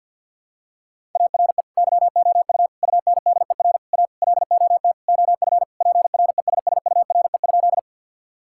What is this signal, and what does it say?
Morse code, heard as a keyed tone, about 52 wpm, 700 Hz. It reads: ARE YOU UNDER A LOT OF PRESSURE?
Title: Sentences from Top 1000 Words - Review Part 6 - 50wpm